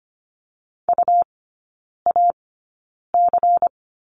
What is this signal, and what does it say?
Morse code, heard as a keyed tone, about 25 wpm, spaced as for 9 wpm, 700 Hz. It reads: UAC
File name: course / 25:10wpm / 100 Lesson 17e - Letter u - Three Characters 25wpm